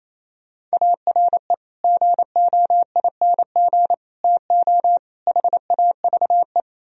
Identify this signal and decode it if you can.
Morse code, heard as a keyed tone, about 37 wpm, 700 Hz. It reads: ARE GOING TO HAVE